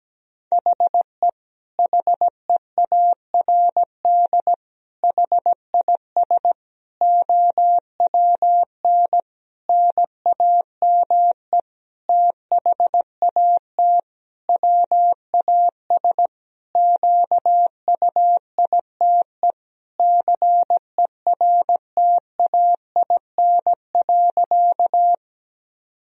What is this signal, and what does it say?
Morse code, heard as a keyed tone, about 17 wpm, 700 Hz. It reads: HE HEARD HIS OWN NAME THAT WAS QUITE CERTAIN.
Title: Sentences from Top 1000 Words - Review Part 2 - 17wpm